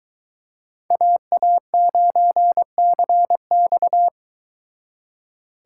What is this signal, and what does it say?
Morse code, heard as a keyed tone, about 23 wpm, 700 Hz. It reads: AA9CX